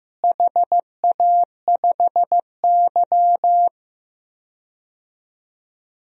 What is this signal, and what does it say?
Morse code, heard as a keyed tone, about 15 wpm, 700 Hz. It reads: HA5Y
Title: DXCC-100 Call Signs - 4 Characters - Repeated 3x - 15wpm